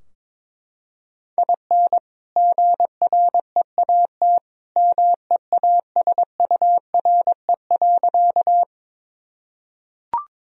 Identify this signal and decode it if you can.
Morse code, heard as a keyed tone, about 22 wpm, 700 Hz. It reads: IN GREAT MEASURE.
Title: Sentences from Top 1000 Words - Review Part 2 - 22wpm